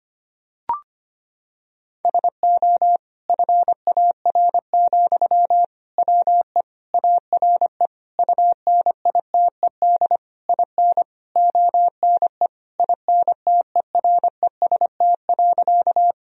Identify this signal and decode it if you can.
Morse code, heard as a keyed tone, about 25 wpm, 700 Hz. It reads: SO FAR, WE ARE UNITED IN ONE INTEREST.
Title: Sentences from Top 1000 Words - Review Part 4 - 25wpm